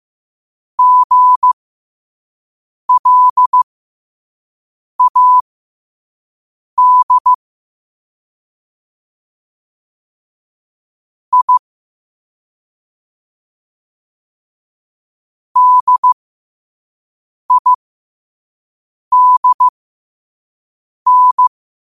Morse code, heard as a keyed tone, about 15 wpm, spaced as for 5 wpm, 1000 Hz. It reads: GLAD I DIDN
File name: qotdfiles / 5wpm